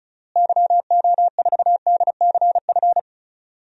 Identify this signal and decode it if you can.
Morse code, heard as a keyed tone, about 34 wpm, 700 Hz. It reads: YO4DCF
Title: DXCC-100 Call Signs - 6 Characters - Repeated 2x - 35wpm